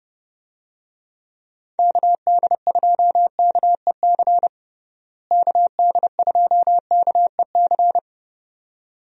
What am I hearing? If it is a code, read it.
Morse code, heard as a keyed tone, about 30 wpm, 700 Hz. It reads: KD2KEC KD2KEC
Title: USA Call Signs - 6 Characters - Repeated 2x - 30wpm